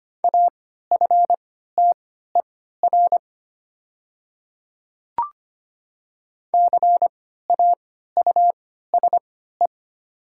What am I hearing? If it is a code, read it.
Morse code, heard as a keyed tone, about 25 wpm, spaced as for 14 wpm, 700 Hz. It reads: AFTER CAUSE
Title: 3x Character Spacing - Top 200 Words - 25wpm